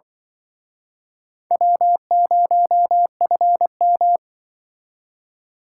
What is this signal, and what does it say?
Morse code, heard as a keyed tone, about 24 wpm, 700 Hz. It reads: W0FM
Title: Speed Racing - USA 1x2 Call Signs 20wpm